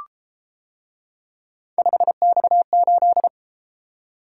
Morse code, heard as a keyed tone, about 33 wpm, 700 Hz. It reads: 5X8